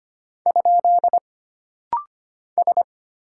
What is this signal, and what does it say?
Morse code, heard as a keyed tone, about 25 wpm, 700 Hz. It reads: ? S
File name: Single Character (Letter:Number:Punctuation:Prosign) - Rapid-Fire - 25wpm